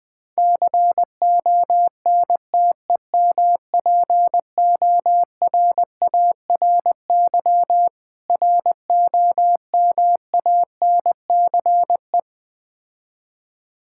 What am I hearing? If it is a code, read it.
Morse code, heard as a keyed tone, about 20 wpm, 690 Hz. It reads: CONTEMPORARY ROMANCE
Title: Difficult Words with Hints - 20wpm